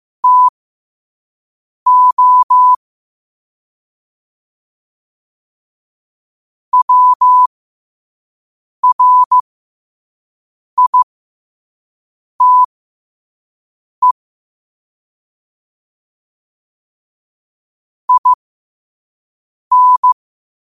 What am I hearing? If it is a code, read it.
Morse code, heard as a keyed tone, about 15 wpm, spaced as for 5 wpm, 1000 Hz. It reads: TO WRITE IN
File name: qotdfiles / 5wpm